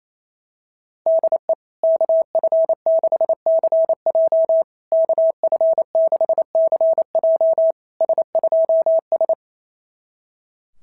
Morse code, heard as a keyed tone, about 29 wpm, 660 Hz. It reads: DE KF6CJ KF6CJ S2S
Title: SOTA QSO Practice 28wpm